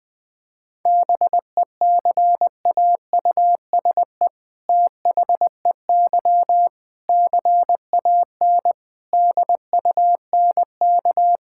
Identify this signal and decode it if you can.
Morse code, heard as a keyed tone, about 20 wpm, 700 Hz. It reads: BECAUSE THEY CAN DUNK
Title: All-Age Appropriate Jokes - 20wpm